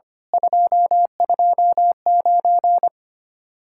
Morse code, heard as a keyed tone, about 30 wpm, 700 Hz. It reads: 229
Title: Speed-Racing Common QSO Phrases Encore - 25wpm